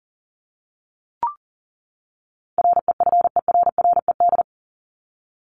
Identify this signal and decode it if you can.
Morse code, heard as a keyed tone, about 40 wpm, 700 Hz. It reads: REFERRED
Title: ED Suffix - Top 100 Words with Base Words 40wpm